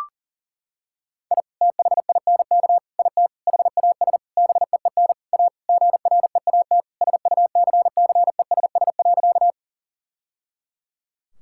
Morse code, heard as a keyed tone, about 37 wpm, 700 Hz. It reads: I THINK IT HAS BEEN A GREAT SUCCESS.
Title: Sentences from Top 1000 Words - Review Part 6 - 40wpm